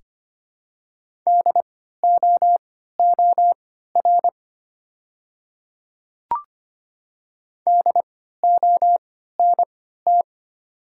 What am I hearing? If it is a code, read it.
Morse code, heard as a keyed tone, about 25 wpm, spaced as for 14 wpm, 700 Hz. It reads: DOOR DONT
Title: 3x Character Spacing - Top 300 Words - 25wpm